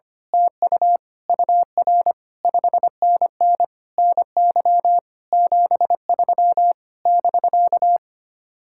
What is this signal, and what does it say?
Morse code, heard as a keyed tone, about 24 wpm, 700 Hz. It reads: TU UR 5NN NY 73 <BK>
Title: Speed-Racing Common QSO Phrases - 25wpm